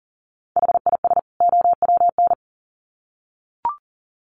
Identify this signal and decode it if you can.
Morse code, heard as a keyed tone, about 40 wpm, 700 Hz. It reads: HIS OWN
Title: Sets of 2 Words 40wpm